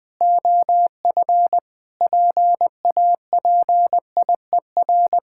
Morse code, heard as a keyed tone, about 20 wpm, 700 Hz. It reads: OF PAPIER
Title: Pangram Challenge v2 - 20wpm